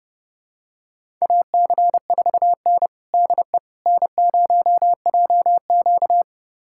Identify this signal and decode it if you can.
Morse code, heard as a keyed tone, about 30 wpm, 700 Hz. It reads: AC4N DE N0JQ
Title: USA Callsign DE Callsign - 4 Characters - 30wpm